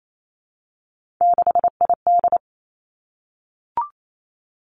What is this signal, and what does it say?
Morse code, heard as a keyed tone, about 28 wpm, 700 Hz. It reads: 6ID
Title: Three Challenging Letter-Number Combinations 28wpm